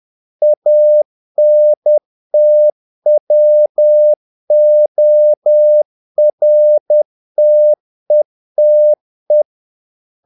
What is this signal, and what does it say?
Morse code, heard as a keyed tone, about 10 wpm, 600 Hz. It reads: ANTWORTETE
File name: dk5ke / Humor-50BpM